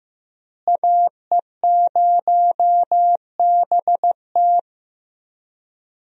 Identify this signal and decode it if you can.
Morse code, heard as a keyed tone, about 16 wpm, 700 Hz. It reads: AE0BT
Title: USA Call Signs - 5 Characters - Repeated 2x - 15wpm